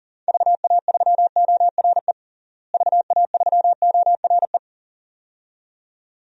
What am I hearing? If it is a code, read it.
Morse code, heard as a keyed tone, about 40 wpm, 700 Hz. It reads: VA3ORE VA3ORE
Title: DXCC-100 Call Signs - 6 Characters - Repeated 2x - 40wpm